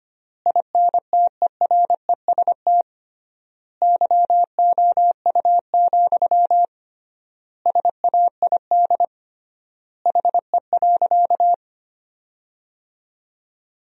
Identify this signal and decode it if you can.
Morse code, heard as a keyed tone, about 25 wpm, 700 Hz. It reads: INTEREST YOU, SAID HE.
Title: 3x Word-Spacing - Sentences from Top 400 Words - 25wpm